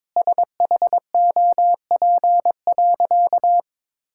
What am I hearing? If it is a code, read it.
Morse code, heard as a keyed tone, about 22 wpm, 700 Hz. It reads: SHOP.
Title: Sentences from Top 1600 Words 22wpm